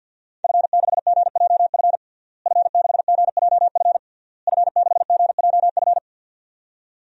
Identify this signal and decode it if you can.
Morse code, heard as a keyed tone, about 53 wpm, 700 Hz. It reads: F6GJF F6GJF F6GJF
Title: DXCC-100 Call Signs - 5 Characters - Repeated 3x - 50wpm